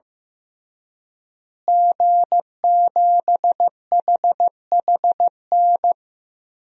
Morse code, heard as a keyed tone, about 15 wpm, 700 Hz. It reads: G7HHN
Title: DXCC-100 Call Signs - 5 Characters - Repeated 2x - 15wpm